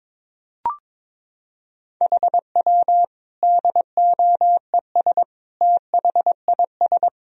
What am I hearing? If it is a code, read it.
Morse code, heard as a keyed tone, about 22 wpm, 700 Hz. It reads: HW DOES THIS
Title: Sentences from Top 400 Words and Abbreviations 22wpm